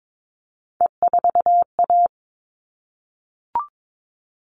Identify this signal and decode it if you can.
Morse code, heard as a keyed tone, about 22 wpm, 700 Hz. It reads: E4A
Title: All Cut Numbers - 3-Digit 22wpm